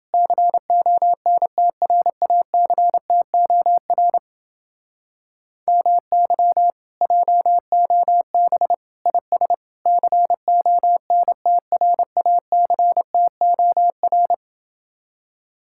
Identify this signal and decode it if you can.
Morse code, heard as a keyed tone, about 30 wpm, 700 Hz. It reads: CONTRACTOR MY JOB IS CONTRACTOR